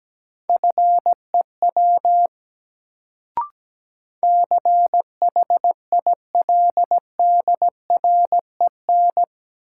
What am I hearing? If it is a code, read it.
Morse code, heard as a keyed tone, about 17 wpm, 700 Hz. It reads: FEW CHILDREN
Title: Sentences from Top 300 Words - New Word Drill - 17wpm